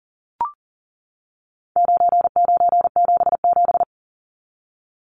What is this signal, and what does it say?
Morse code, heard as a keyed tone, about 40 wpm, 700 Hz. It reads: 9977
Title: Numbers - 4-Digit 40wpm